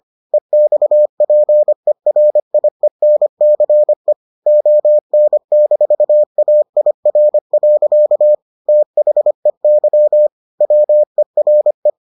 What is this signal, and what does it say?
Morse code, heard as a keyed tone, about 25 wpm, 600 Hz. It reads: EXPERIENCE ON-AIR. THEY WERE